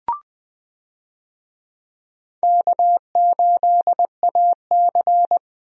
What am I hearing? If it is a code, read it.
Morse code, heard as a keyed tone, about 20 wpm, 700 Hz. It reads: K8AC
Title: USA Call Signs - 4 Characters - Repeated 2x - 20wpm